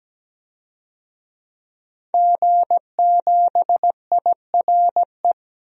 Morse code, heard as a keyed tone, about 17 wpm, 700 Hz. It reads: G7IRE
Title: DXCC-100 Call Signs - 5 Characters - Repeated 2x - 17wpm